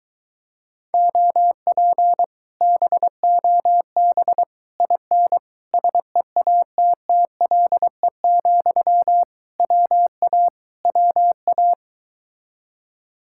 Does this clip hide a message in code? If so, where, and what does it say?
Yes: Morse code from 0.9 to 11.7 s, heard as a keyed tone, about 23 wpm, 700 Hz. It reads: OP BOB IN SEATTLE, WA WA